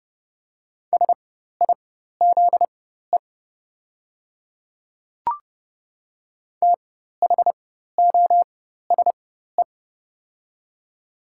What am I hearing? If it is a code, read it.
Morse code, heard as a keyed tone, about 30 wpm, spaced as for 14 wpm, 700 Hz. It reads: SIZE THOSE
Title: 4x Character Spacing - Top 500 Words - 30wpm